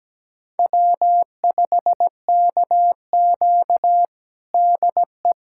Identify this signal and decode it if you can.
Morse code, heard as a keyed tone, about 17 wpm, 700 Hz. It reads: W5KQ DE
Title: USA Callsign DE Callsign - 4 Characters - 17wpm